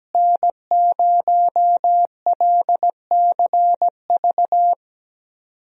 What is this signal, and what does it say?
Morse code, heard as a keyed tone, about 17 wpm, 700 Hz. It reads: N0LCV
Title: USA Call Signs - 5 Characters - Repeated 3x - 17wpm